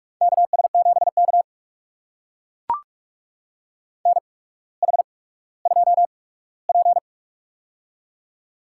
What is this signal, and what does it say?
Morse code, heard as a keyed tone, about 44 wpm, spaced as for 19 wpm, 700 Hz. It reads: KS7K NH2P
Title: USA Call Signs - 8x Character Spacing - 45wpm